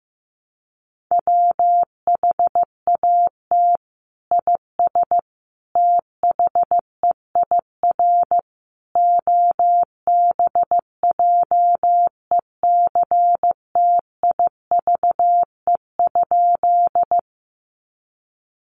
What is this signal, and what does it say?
Morse code, heard as a keyed tone, about 15 wpm, 700 Hz. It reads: WHAT IS THEIR OBJECTIVE?
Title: Sentences from Top 1200 Words 15wpm